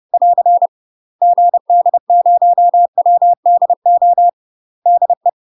Morse code, heard as a keyed tone, about 30 wpm, 700 Hz. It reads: <AR> GD0WDO DE
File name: UK MASTER QSOs TRAINING / 30WPM / UK MASTER QSOs 0040